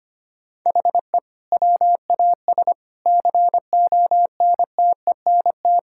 Morse code, heard as a keyed tone, about 25 wpm, 700 Hz. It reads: HE WAS CONTENT
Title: Sentences from Top 1000 Words - Review Part 6 - 25wpm